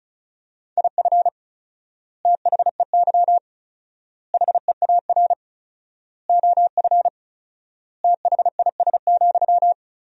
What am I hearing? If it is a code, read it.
Morse code, heard as a keyed tone, about 36 wpm, 700 Hz. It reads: IF THEY HEAR OF THIS,
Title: 4x Word-Spacing - Sentences from Top 400 Words - 35wpm